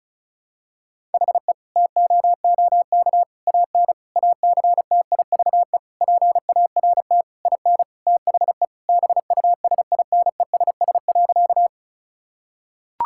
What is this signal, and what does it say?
Morse code, heard as a keyed tone, about 36 wpm, 700 Hz. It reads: HE TOOK AN ACTIVE PART IN THE BUSINESS.
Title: Sentences from Top 1000 Words - Review Part 6 - 35wpm